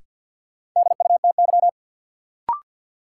Morse code, heard as a keyed tone, about 53 wpm, 700 Hz. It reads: DUTY
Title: Sentences from Top 1500 Words - New Word Drill Alt - 50wpm